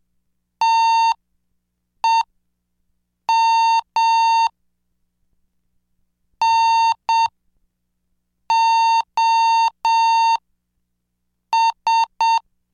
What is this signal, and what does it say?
Morse code, heard as a keyed tone, about 7 wpm, spaced as for 5 wpm, 910 Hz. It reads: TEM NOS
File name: cursos / CDestudo / 26 Track 26